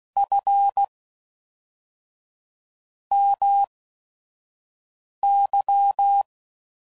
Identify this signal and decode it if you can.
Morse code, heard as a keyed tone, about 16 wpm, spaced as for 5 wpm, 800 Hz. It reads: F MY
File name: CW Code Groups @5WPM / QSOs / WD0FBT 2